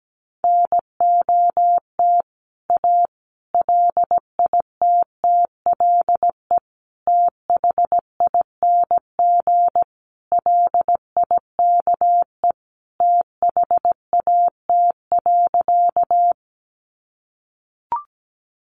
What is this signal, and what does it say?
Morse code, heard as a keyed tone, about 17 wpm, 700 Hz. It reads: NOT A LITTLE THING LIKE THAT.
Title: A Sound Of Thunder 17wpm